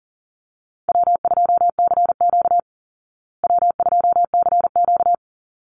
Morse code, heard as a keyed tone, about 40 wpm, 700 Hz. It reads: W2CQ W2CQ
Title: Easy Call Signs Repeated 2x - 40wpm